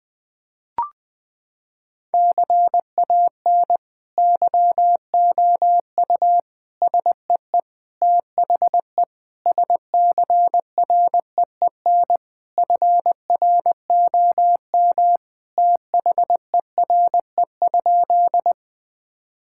Morse code, heard as a keyed tone, about 20 wpm, 700 Hz. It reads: CAN YOU SEE THE SCREEN FROM THERE?
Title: Sentences from Top 1000 Words - Review Part 6 - 20wpm